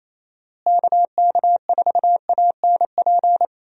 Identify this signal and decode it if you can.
Morse code, heard as a keyed tone, about 28 wpm, 700 Hz. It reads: KK4ANP
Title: USA Call Signs - 6 Characters - Repeated 2x - 28wpm